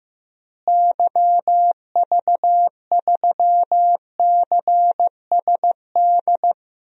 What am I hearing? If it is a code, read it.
Morse code, heard as a keyed tone, about 15 wpm, 700 Hz. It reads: YV3CSD
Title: DXCC-100 Call Signs - 6 Characters - Repeated 2x - 15wpm